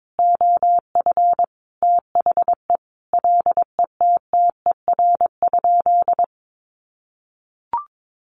Morse code, heard as a keyed tone, about 22 wpm, 700 Hz. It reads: OF THE LETTER?
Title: Sentences from Top 1300 Words 22wpm